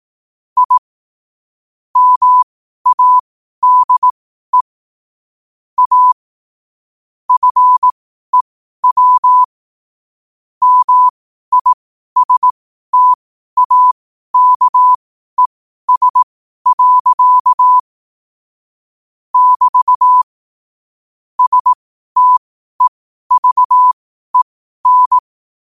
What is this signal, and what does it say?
Morse code, heard as a keyed tone, about 18 wpm, spaced as for 12 wpm, 1000 Hz. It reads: I MADE A FEW MISTAKES. <BT> STEVEN
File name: qotdfiles / 12wpm